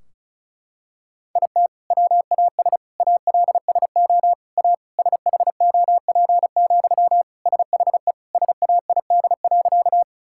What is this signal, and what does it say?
Morse code, heard as a keyed tone, about 36 wpm, 700 Hz. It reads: IT WAS ALSO A SHOP, SHE SAID.
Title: Sentences from Top 1600 Words 35wpm